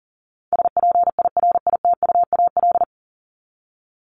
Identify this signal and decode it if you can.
Morse code, heard as a keyed tone, about 40 wpm, 700 Hz. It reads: SPIRITUAL
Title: Top 1501-1750 Words 40wpm